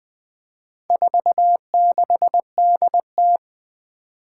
Morse code, heard as a keyed tone, about 20 wpm, 700 Hz. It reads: 46DT
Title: Four Challenging Letter-Number Combinations 20wpm